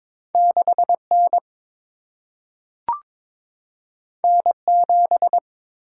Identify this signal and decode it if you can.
Morse code, heard as a keyed tone, about 22 wpm, 700 Hz. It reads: 6 N N 7
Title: Single-Letter Single-Number 22wpm